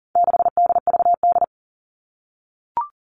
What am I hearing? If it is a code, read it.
Morse code, heard as a keyed tone, about 40 wpm, 700 Hz. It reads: 6DVD
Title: All Cut Numbers - 4-Digit 40wpm